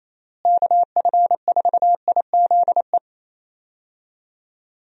Morse code, heard as a keyed tone, about 28 wpm, 700 Hz. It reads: KF4IZE
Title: USA Call Signs - 6 Characters - Repeated 2x - 28wpm